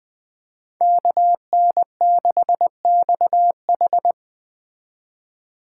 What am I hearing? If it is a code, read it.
Morse code, heard as a keyed tone, about 20 wpm, 700 Hz. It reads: KN6XH
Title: Speed Racing - USA Call Signs 17wpm